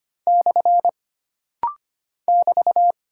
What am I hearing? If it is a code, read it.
Morse code, heard as a keyed tone, about 25 wpm, 700 Hz. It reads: / <BT>
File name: Single Punctuation-Prosign - Rapid-Fire - 25wpm